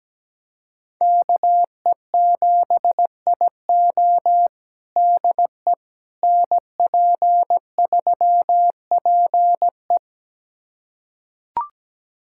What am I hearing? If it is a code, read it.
Morse code, heard as a keyed tone, about 17 wpm, 700 Hz. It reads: KE7IO DE NP3PE
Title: USA Callsign DE Callsign - 5 Characters - 17wpm